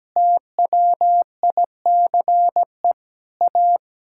Morse code, heard as a keyed tone, about 17 wpm, 700 Hz. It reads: TWICE A
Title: Sentences from Top 1600 Words 17wpm